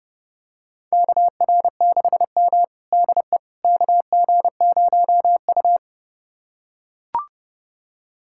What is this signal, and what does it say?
Morse code, heard as a keyed tone, about 32 wpm, 700 Hz. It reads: KR6M DE KG0U
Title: USA Callsign DE Callsign - 4 Characters - 30wpm